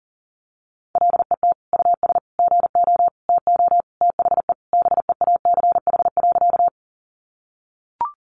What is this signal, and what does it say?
Morse code, heard as a keyed tone, about 40 wpm, 700 Hz. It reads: LET US GO TO THE BEACH.
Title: Sentences from Top 1500 Words 40wpm